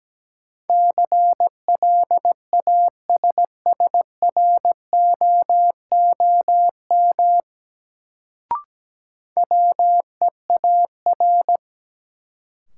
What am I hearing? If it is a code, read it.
Morse code, heard as a keyed tone, about 17 wpm, 700 Hz. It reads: CLASSROOM WEAR
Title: Sentences from Top 1500 Words - New Word Drill Alt - 17wpm